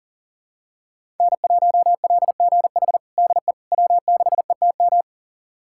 Morse code, heard as a keyed tone, about 40 wpm, 700 Hz. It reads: N1LGH DE W6ETM
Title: USA Callsign DE Callsign - 5 Characters - 40wpm